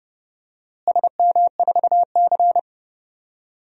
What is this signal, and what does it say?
Morse code, heard as a keyed tone, about 30 wpm, 700 Hz. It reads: SM4C
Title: DXCC-100 Call Signs - 4 Characters - Repeated 2x - 30wpm